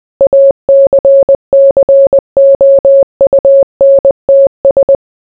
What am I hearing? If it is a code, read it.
Morse code, heard as a keyed tone, about 20 wpm, 550 Hz. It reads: ACCOUNTS